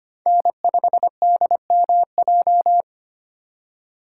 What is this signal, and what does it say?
Morse code, heard as a keyed tone, about 32 wpm, 700 Hz. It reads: N5DMJ